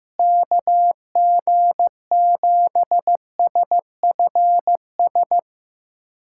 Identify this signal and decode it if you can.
Morse code, heard as a keyed tone, about 16 wpm, 700 Hz. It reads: KG7SFS
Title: USA Callsign DE Callsign 15wpm